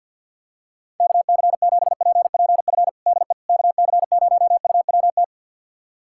Morse code, heard as a keyed tone, about 50 wpm, 700 Hz. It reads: KC7PPF DE KC0UWT
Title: USA Callsign DE Callsign - 6 Characters - 50wpm